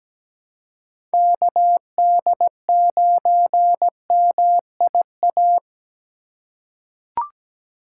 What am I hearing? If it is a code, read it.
Morse code, heard as a keyed tone, about 17 wpm, 700 Hz. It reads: KD9MIA
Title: USA Call Signs - 6 Characters - 4x Character Spacing - 17wpm